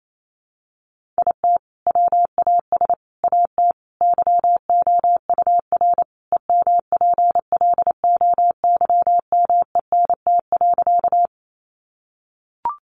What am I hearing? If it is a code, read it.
Morse code, heard as a keyed tone, about 28 wpm, 700 Hz. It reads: IT WAS AT YOUR EMPLOYMENT.
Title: Sentences from Top 1500 Words 28wpm